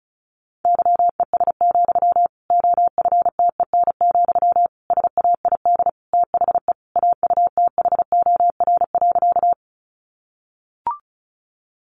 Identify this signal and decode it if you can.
Morse code, heard as a keyed tone, about 34 wpm, 700 Hz. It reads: YES, OFTEN, SAID THE AUTHOR.
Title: Sentences from Top 1200 Words 35wpm